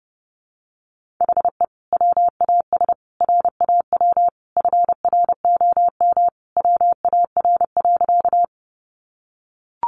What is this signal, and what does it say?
Morse code, heard as a keyed tone, about 30 wpm, 700 Hz. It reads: HE WAS RAW FROM WAR.
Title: Sentences from Top 1500 Words 30wpm